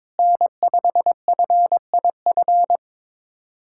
Morse code, heard as a keyed tone, about 22 wpm, 700 Hz. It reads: N5FIF
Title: USA Callsign DE Callsign 22wpm